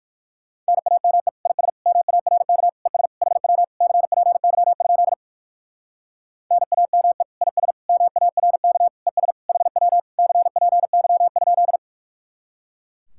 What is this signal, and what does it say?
Morse code, heard as a keyed tone, about 58 wpm, 700 Hz. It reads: NAME IS MARK ES HW CPY? NAME IS MARK ES HW CPY?